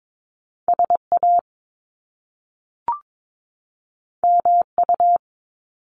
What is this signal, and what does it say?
Morse code, heard as a keyed tone, about 22 wpm, 700 Hz. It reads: SA MU
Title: Top Two Letter Combinations in Words 22wpm